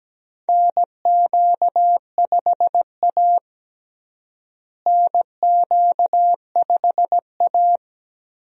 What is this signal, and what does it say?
Morse code, heard as a keyed tone, about 17 wpm, 700 Hz. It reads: NQ5A NQ5A
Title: USA Call Signs - 4 Characters - Repeated 2x - 17wpm